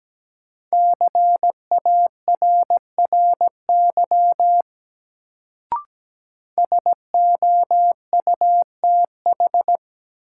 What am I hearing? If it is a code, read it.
Morse code, heard as a keyed tone, about 17 wpm, 700 Hz. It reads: CARRY SOUTH
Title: Sentences from Top 400 Words - New Word Drill - 17wpm